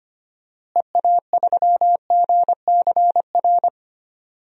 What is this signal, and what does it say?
Morse code, heard as a keyed tone, about 25 wpm, 690 Hz. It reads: EA3GCR
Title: DXCC-100 Call Signs - 6 Characters - Repeated 3x - 25wpm